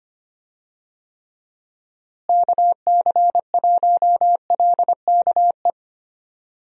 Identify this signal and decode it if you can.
Morse code, heard as a keyed tone, about 25 wpm, 700 Hz. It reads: KC1LKE
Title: USA Call Signs - 6 Characters - Repeated 3x - 25wpm